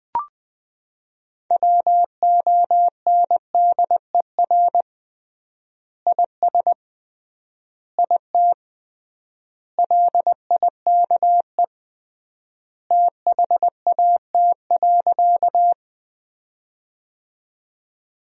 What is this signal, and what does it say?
Morse code, heard as a keyed tone, about 20 wpm, 700 Hz. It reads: WONDER IS IT LIKE THAT.
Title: 3x Word-Spacing - Sentences from Top 500 Words - 20wpm